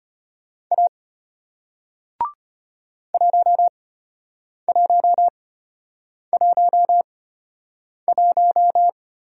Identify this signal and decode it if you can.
Morse code, heard as a keyed tone, about 34 wpm, spaced as for 8 wpm, 700 Hz. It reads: A 1111